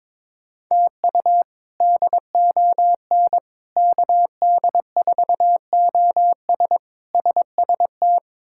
Morse code, heard as a keyed tone, about 22 wpm, 700 Hz. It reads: TU DON KD4OS SST
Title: K1USN SST QSO Practice - 22wpm